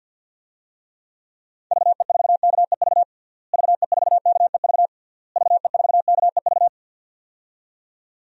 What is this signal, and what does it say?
Morse code, heard as a keyed tone, about 52 wpm, 700 Hz. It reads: VE4KEV VE4KEV VE4KEV
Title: DXCC-100 Call Signs - 6 Characters - Repeated 3x - 50wpm